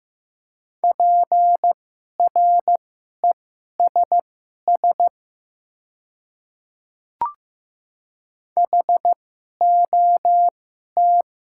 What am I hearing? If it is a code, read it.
Morse code, heard as a keyed tone, about 15 wpm, spaced as for 11 wpm, 700 Hz. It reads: PRESS HOT
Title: 2x Character Spacing - Top 500 Words - 15wpm